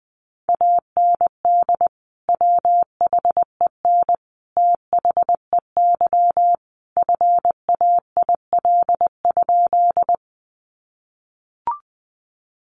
Morse code, heard as a keyed tone, about 20 wpm, 700 Hz. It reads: AND WHEN THEY FAIL?
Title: Sentences from Top 1100 Words 20wpm